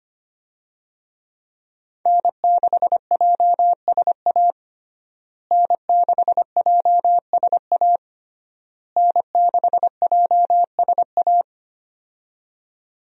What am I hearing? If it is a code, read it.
Morse code, heard as a keyed tone, about 25 wpm, 700 Hz. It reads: N6JSA N6JSA N6JSA
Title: USA Call Signs - 5 Characters - Repeated 3x - 25wpm